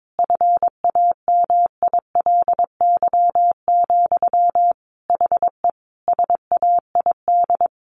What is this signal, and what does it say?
Morse code, heard as a keyed tone, about 22 wpm, 700 Hz. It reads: FAMILY, HE SAID
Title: Sentences from Top 1100 Words 22wpm